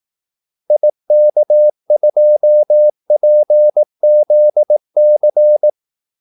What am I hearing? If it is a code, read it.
Morse code, heard as a keyed tone, about 18 wpm, 600 Hz. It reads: IK2PZC
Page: Kaldesignaler 18 wpm